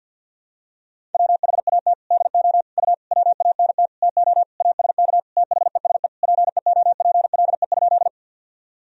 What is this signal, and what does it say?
Morse code, heard as a keyed tone, about 52 wpm, 700 Hz. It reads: WHAT DO U WANT TO ASK THESE PEOPLE?
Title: Sentences from Top 200 Words and Abbreviations 50wpm